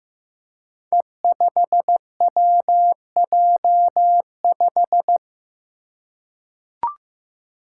Morse code, heard as a keyed tone, about 15 wpm, 700 Hz. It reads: E5WJ5
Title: Five Challenging Letter-Number Combinations 15wpm